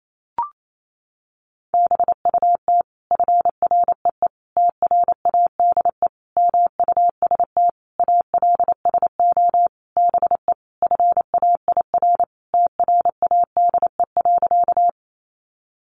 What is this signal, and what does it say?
Morse code, heard as a keyed tone, about 28 wpm, 700 Hz. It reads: BUT FREE TRADE MUST ALSO BE FAIR TRADE.
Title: Sentences from Top 1300 Words 28wpm